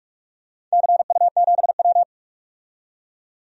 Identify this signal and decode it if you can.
Morse code, heard as a keyed tone, about 44 wpm, 690 Hz. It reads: CU7W
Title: DXCC-100 Call Signs - 4 Characters - 45wpm